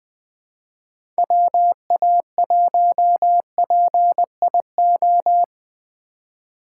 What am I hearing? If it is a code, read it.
Morse code, heard as a keyed tone, about 20 wpm, 700 Hz. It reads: WA1PIO
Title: Speed Racing - USA 2x3 Call Signs 17wpm